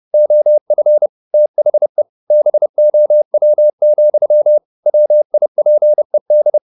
Morse code, heard as a keyed tone, about 30 wpm, 600 Hz. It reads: OF THE BOW, WIPED
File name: the old man and the sea / the old man and the sea 1 30wpm